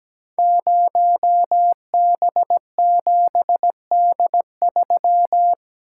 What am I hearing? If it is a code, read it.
Morse code, heard as a keyed tone, about 17 wpm, 700 Hz. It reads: 0B7D3
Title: Code Groups - 5 Letter:Numbers - 17wpm